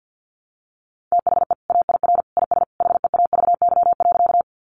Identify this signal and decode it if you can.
Morse code, heard as a keyed tone, about 50 wpm, 700 Hz. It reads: THE AIR IS HEAVY.
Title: Sentences from Top 1100 Words 50wpm